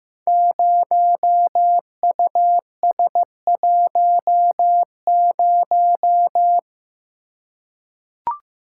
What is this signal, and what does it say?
Morse code, heard as a keyed tone, about 15 wpm, 700 Hz. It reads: 0US10
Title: Code Groups - 5 Letter:Numbers - 15wpm